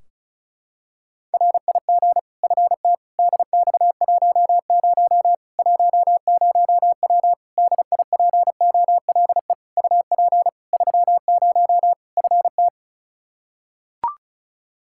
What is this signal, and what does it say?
Morse code, heard as a keyed tone, about 36 wpm, 700 Hz. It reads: RIG FT DX10 10W DIPOLE UP 30 FT
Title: Practice QSO - Exchange 2 - Info - 35wpm